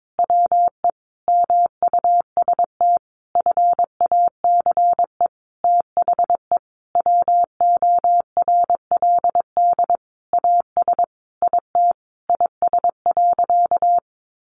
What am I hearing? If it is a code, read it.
Morse code, heard as a keyed tone, about 22 wpm, 700 Hz. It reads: WE MUST FACE THE WORLD AS IT IS.
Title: Sentences from Top 1200 Words 22wpm